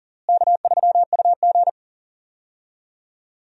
Morse code, heard as a keyed tone, about 40 wpm, 700 Hz. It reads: K3UG
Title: USA Call Signs - 4 Characters - Repeated 3x - 40wpm